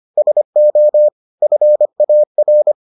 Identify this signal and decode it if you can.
Morse code, heard as a keyed tone, about 25 wpm, 600 Hz. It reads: SO FAR